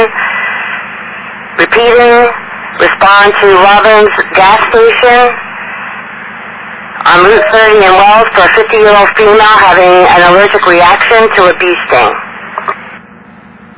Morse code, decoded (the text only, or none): none